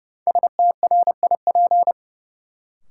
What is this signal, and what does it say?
Morse code, heard as a keyed tone, about 30 wpm, 700 Hz. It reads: STRIP
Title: Sentences from Top 1600 Words - New Word Drill - 30wpm